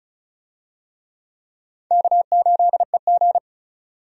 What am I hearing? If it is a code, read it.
Morse code, heard as a keyed tone, about 36 wpm, 700 Hz. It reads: K8EG
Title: USA Call Signs - 4 Characters - 35wpm